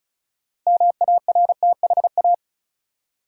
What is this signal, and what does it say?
Morse code, heard as a keyed tone, about 34 wpm, 700 Hz. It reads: MARTHA
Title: USA Common Names - 35wpm